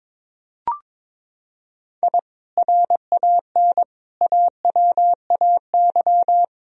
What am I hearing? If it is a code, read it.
Morse code, heard as a keyed tone, about 22 wpm, 700 Hz. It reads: I RAN AWAY
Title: Sentences from Top 1000 Words - Review Part 3 - 22wpm